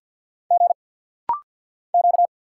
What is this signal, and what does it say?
Morse code, heard as a keyed tone, about 50 wpm, 690 Hz. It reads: G Q
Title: Single Letter-Number Alt - Rapid-Fire 50wpm